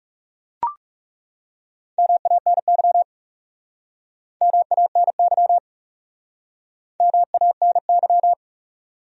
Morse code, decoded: MANY MANY MANY

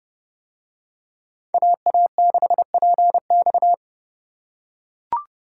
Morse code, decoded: AA6PX

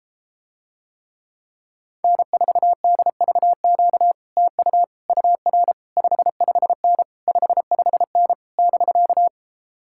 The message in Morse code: N4DVQ TU UR 55N 55N <BK>

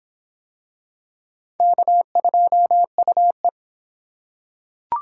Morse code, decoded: K2UE

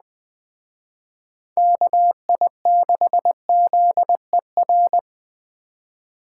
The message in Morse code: KI6ZER

